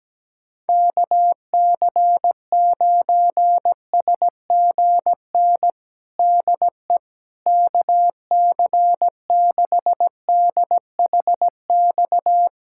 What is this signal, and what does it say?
Morse code, heard as a keyed tone, about 17 wpm, 700 Hz. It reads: KC9SGN DE KC6DHX
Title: USA Callsign DE Callsign 17wpm